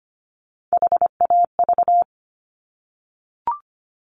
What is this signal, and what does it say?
Morse code, heard as a keyed tone, about 25 wpm, 700 Hz. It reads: HAV
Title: Top Three Letter Combinations in Words 25wpm